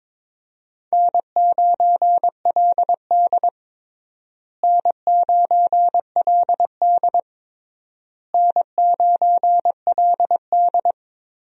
Morse code, heard as a keyed tone, about 22 wpm, 700 Hz. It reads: N9LD N9LD N9LD
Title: USA Call Signs - 4 Characters - Repeated 3x - 22wpm